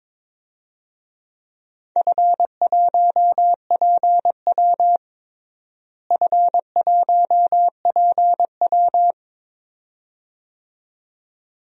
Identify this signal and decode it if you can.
Morse code, heard as a keyed tone, about 22 wpm, 700 Hz. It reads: F1PW F1PW
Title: DXCC-100 Call Signs - 4 Characters - Repeated 2x - 22wpm